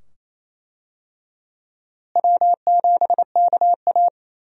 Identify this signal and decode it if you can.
Morse code, heard as a keyed tone, about 28 wpm, 700 Hz. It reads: W7KA
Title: USA Call Signs - 4 Characters - Repeated 2x - 28wpm